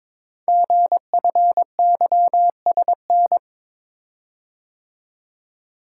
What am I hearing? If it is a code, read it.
Morse code, heard as a keyed tone, about 23 wpm, 700 Hz. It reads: GFYSN
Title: Code Groups - 5 Letter:Numbers - 22wpm